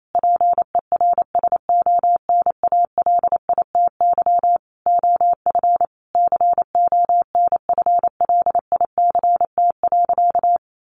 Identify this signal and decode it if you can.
Morse code, heard as a keyed tone, about 28 wpm, 700 Hz. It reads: PERSONALITY OF CONFLICT.
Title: Sentences from Top 1400 Words 28wpm